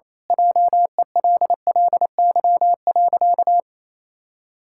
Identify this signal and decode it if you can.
Morse code, heard as a keyed tone, about 28 wpm, 700 Hz. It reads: JELLY.
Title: Pangram Challenge v2 - 28wpm